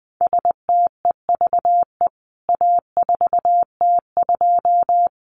Morse code, heard as a keyed tone, about 20 wpm, 700 Hz. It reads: STEVE A4T2
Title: CWT Exchange 20wpm